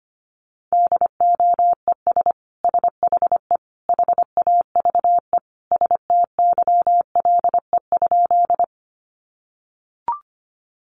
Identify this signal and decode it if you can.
Morse code, heard as a keyed tone, about 25 wpm, 700 Hz. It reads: DOES SHE HAVE STYLE?
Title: Sentences from Top 800 Words 25wpm